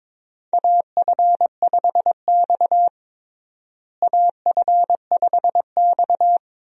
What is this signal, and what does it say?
Morse code, heard as a keyed tone, about 22 wpm, 700 Hz. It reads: AF5X AF5X
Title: USA Call Signs - 4 Characters - Repeated 3x - 22wpm